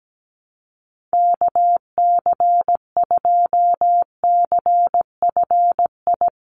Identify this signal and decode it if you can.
Morse code, heard as a keyed tone, about 17 wpm, 700 Hz. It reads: KC2CFI
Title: USA Call Signs 17wpm